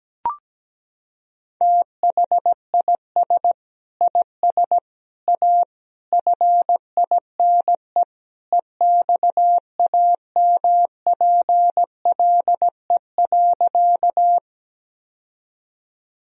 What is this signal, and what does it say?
Morse code, heard as a keyed tone, about 17 wpm, 700 Hz. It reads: THIS IS A FINE EXAMPLE.
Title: Sentences from Top 1000 Words - Review Part 1 - 17wpm